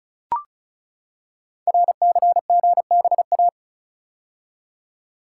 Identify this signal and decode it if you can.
Morse code, heard as a keyed tone, about 36 wpm, 700 Hz. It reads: RCGBA
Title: Code Groups - 5 Letters - 35wpm